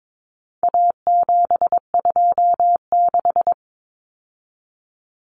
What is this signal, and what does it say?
Morse code, heard as a keyed tone, about 22 wpm, 700 Hz. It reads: A726
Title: Common Cut Numbers - 4-Digit 22wpm